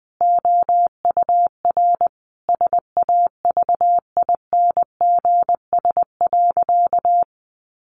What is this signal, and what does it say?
Morse code, heard as a keyed tone, about 19 wpm, 700 Hz. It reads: OUR SAVINGS.
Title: Sentences from Top 1400 Words 20wpm